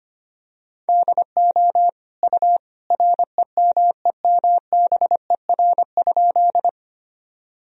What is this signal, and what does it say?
Morse code, heard as a keyed tone, about 25 wpm, 700 Hz. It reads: DO U REMEMBER?
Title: Sentences from Top 400 Words and Abbreviations 25wpm